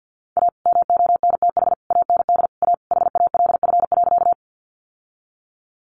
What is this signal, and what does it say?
Morse code, heard as a keyed tone, about 50 wpm, 700 Hz. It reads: A MONTH AND A HALF.
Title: Sentences from Top 800 Words 50wpm